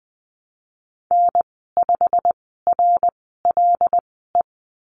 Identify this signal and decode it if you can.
Morse code, heard as a keyed tone, about 20 wpm, spaced as for 15 wpm, 700 Hz. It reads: N5RLE
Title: 088 Lesson 15g - Number 5 - Call Signs 20wpm